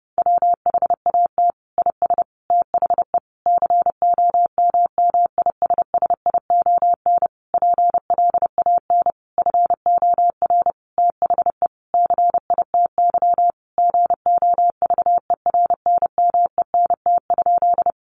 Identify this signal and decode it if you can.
Morse code, heard as a keyed tone, about 28 wpm, 700 Hz. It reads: WHAT IS THE COMMISSION PLAN FOR THE CITY GOVERNMENT?
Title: Sentences from Top 1400 Words 30wpm